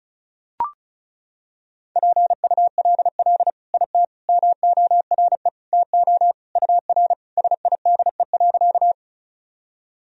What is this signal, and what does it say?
Morse code, heard as a keyed tone, about 36 wpm, 700 Hz. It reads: PULL IT MORE TO UR SIDE.
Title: Sentences from Top 400 Words and Abbreviations 35wpm